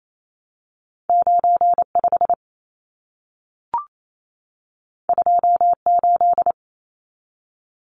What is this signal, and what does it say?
Morse code, heard as a keyed tone, about 28 wpm, 700 Hz. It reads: 95 28
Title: Numbers - 2-Digit 28wpm